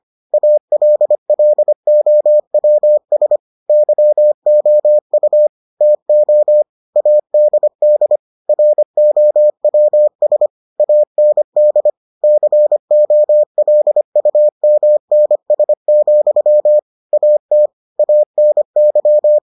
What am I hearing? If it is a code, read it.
Morse code, heard as a keyed tone, about 25 wpm, 600 Hz. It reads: ALLOWS YOU TO ADD ROWS AND COLUMNS, AT ANY